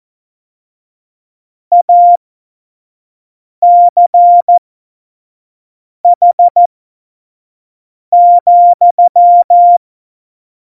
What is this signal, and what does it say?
Morse code, heard as a keyed tone, about 14 wpm, spaced as for 5 wpm, 700 Hz. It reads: ACH,